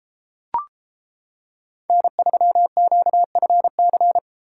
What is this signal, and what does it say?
Morse code, heard as a keyed tone, about 33 wpm, 700 Hz. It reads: N3QFC